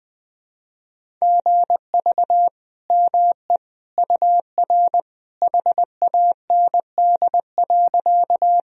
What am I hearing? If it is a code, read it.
Morse code, heard as a keyed tone, about 20 wpm, 700 Hz. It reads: GV ME UR HAND.
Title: Sentences from Top 200 Words and Abbreviations 20wpm